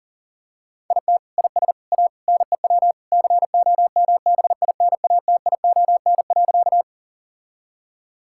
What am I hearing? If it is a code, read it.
Morse code, heard as a keyed tone, about 37 wpm, 700 Hz. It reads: IT IS A NEW COMBINATION.
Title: Sentences from Top 1000 Words - Review Part 6 - 40wpm